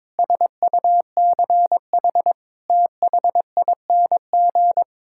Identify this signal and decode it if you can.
Morse code, heard as a keyed tone, about 22 wpm, 700 Hz. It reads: SUCH THING
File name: practice / 22wpm / Sets of 6 Words - 22wpm